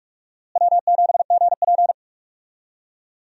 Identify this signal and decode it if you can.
Morse code, heard as a keyed tone, about 46 wpm, 700 Hz. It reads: W7GP
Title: USA Call Signs - 4 Characters - Repeated 3x - 45wpm